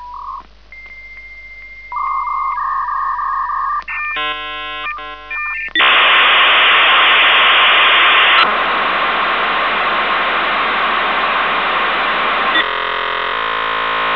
Dial-up modem handshake.
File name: ea / V.90 - RockWeller - 56k